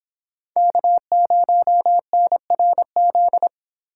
Morse code, heard as a keyed tone, about 25 wpm, 700 Hz. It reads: K0NRZ